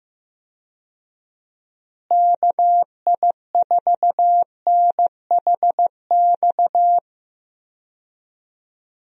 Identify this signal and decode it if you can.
Morse code, heard as a keyed tone, about 15 wpm, 700 Hz. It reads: KI4NHX
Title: USA Call Signs - 6 Characters - Repeated 2x - 15wpm